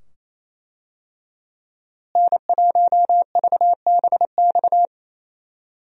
Morse code, heard as a keyed tone, about 28 wpm, 700 Hz. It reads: N1VBX
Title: USA Call Signs - 5 Characters - Repeated 2x - 28wpm